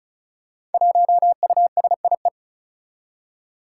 Morse code, heard as a keyed tone, about 36 wpm, 700 Hz. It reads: 1USIE
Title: Five Challenging Letter-Number Combinations 35wpm